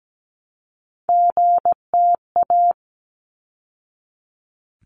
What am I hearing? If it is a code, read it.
Morse code, heard as a keyed tone, about 17 wpm, 700 Hz. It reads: GTA
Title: ARRL Sections 17wpm